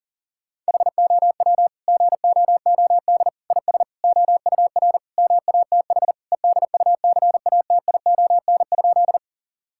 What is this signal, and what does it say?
Morse code, heard as a keyed tone, about 40 wpm, 700 Hz. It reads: HOW GOOD IS OUR MATH EDUCATION?
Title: Sentences from Top 1000 Words - Review Part 6 - 40wpm